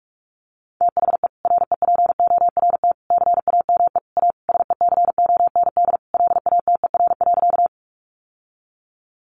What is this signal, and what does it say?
Morse code, heard as a keyed tone, about 44 wpm, 700 Hz. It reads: THE REPORT CAME A SECOND LATER.
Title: Sentences from Top 900 Words 45wpm